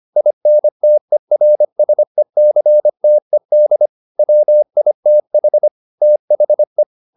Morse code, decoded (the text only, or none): INTERSECTED WITH THE